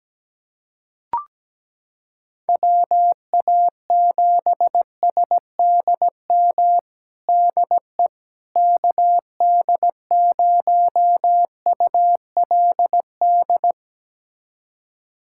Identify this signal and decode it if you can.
Morse code, heard as a keyed tone, about 17 wpm, 700 Hz. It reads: WA7SDM DE KD0ULD
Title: USA Callsign DE Callsign - 6 Characters - 17wpm